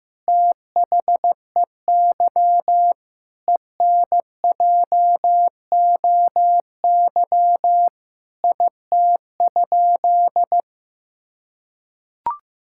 Morse code, heard as a keyed tone, about 16 wpm, 700 Hz. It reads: THEY ENJOY IT?
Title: Sentences from Top 1000 Words - Review Part 5 - 15wpm